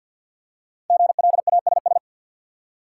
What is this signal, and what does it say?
Morse code, heard as a keyed tone, about 51 wpm, 700 Hz. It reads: GLASS
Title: Five Letter Words - 50wpm